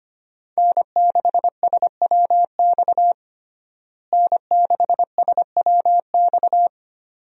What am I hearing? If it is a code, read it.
Morse code, heard as a keyed tone, about 25 wpm, 700 Hz. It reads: N6SWX N6SWX
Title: USA Call Signs - 5 Characters - Repeated 2x - 25wpm